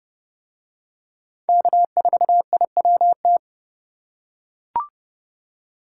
Morse code, heard as a keyed tone, about 27 wpm, 700 Hz. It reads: K4IWT